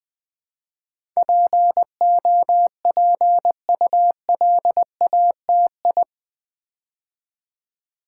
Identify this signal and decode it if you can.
Morse code, heard as a keyed tone, about 20 wpm, 700 Hz. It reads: POPULATI
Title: Build A Word - 4 Letters At A Time - 10-Letter Words - 20wpm